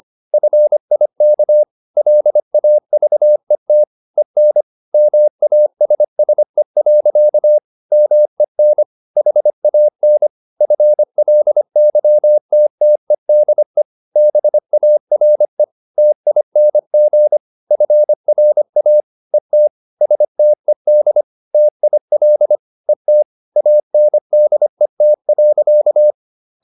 Morse code, heard as a keyed tone, about 25 wpm, 600 Hz. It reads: FIK LAVET EN MASSE. MEN HAN FLYTTEDE BARE TING FRA ET STED TIL ET ANDET.
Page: Ferie 25 wpm